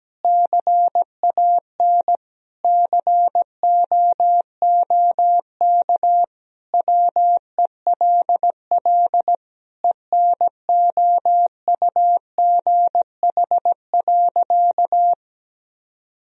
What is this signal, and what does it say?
Morse code, heard as a keyed tone, about 17 wpm, 700 Hz. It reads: CAN COOK WELL ENOUGH.
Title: Sentences from Top 1000 Words - Review Part 4 - 17wpm